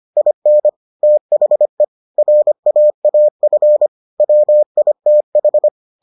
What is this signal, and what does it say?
Morse code, heard as a keyed tone, about 25 wpm, 600 Hz. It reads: IN THE RAAF WITH